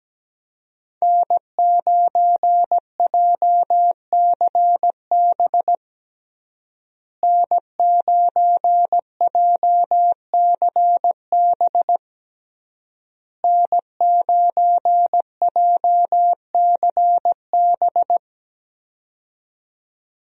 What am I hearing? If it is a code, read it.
Morse code, heard as a keyed tone, about 17 wpm, 700 Hz. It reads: N9JCB N9JCB N9JCB
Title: USA Call Signs - 5 Characters - Repeated 3x - 17wpm